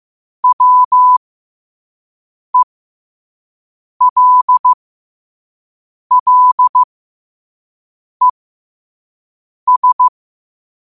Morse code, heard as a keyed tone, about 15 wpm, spaced as for 5 wpm, 1000 Hz. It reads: WELLES
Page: New quotes every day in morse code at 5 Words per minute.